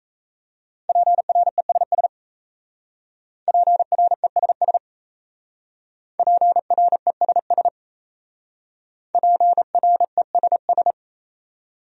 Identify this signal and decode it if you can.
Morse code, heard as a keyed tone, about 35 wpm, 700 Hz. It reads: PRESS PRESS PRESS PRESS
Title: Speed-Racing Top 300 Words 28wpm